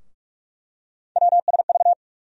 Morse code, heard as a keyed tone, about 46 wpm, 700 Hz. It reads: WSV
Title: USA Call Sign Postfixes - 3 Letters 45wpm